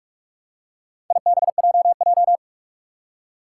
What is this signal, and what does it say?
Morse code, heard as a keyed tone, about 42 wpm, 700 Hz. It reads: IBJJ
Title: Four Challenging Letter-Number Combinations 45wpm